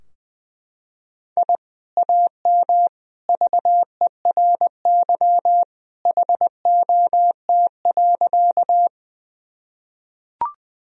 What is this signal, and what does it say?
Morse code, heard as a keyed tone, about 20 wpm, 700 Hz. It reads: I AM VERY HOT.
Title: Sentences from Top 1000 Words - Review Part 3 - 20wpm